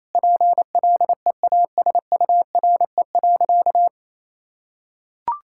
Morse code, heard as a keyed tone, about 28 wpm, 700 Hz. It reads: PLEASURE.
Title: Sentences from Top 1600 Words 28wpm